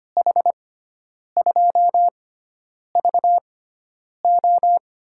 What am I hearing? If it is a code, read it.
Morse code, heard as a keyed tone, about 25 wpm, spaced as for 9 wpm, 700 Hz. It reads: H2VO
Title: USA Call Signs - 5 Characters - 6x Character Spacing - 25wpm